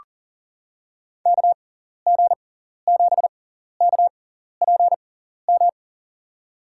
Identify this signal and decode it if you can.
Morse code, heard as a keyed tone, about 40 wpm, spaced as for 14 wpm, 700 Hz. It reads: KG7KPM
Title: USA Call Signs - 6 Characters - 6x Character Spacing - 40wpm